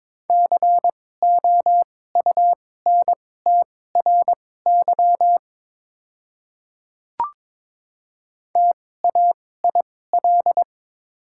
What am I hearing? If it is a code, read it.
Morse code, heard as a keyed tone, about 22 wpm, spaced as for 16 wpm, 700 Hz. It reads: COUNTRY TAIL
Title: 2x Character Spacing - Top 500 Words - 22wpm